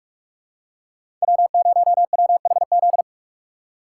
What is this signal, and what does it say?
Morse code, heard as a keyed tone, about 46 wpm, 700 Hz. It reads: W0WHZ
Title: USA Call Signs - 5 Characters - 2x Character Spacing - 45wpm